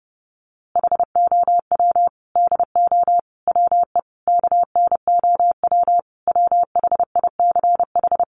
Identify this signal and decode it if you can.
Morse code, heard as a keyed tone, about 30 wpm, 700 Hz. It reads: HOW DO WE KNOW WHICH
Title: A Sound Of Thunder 30wpm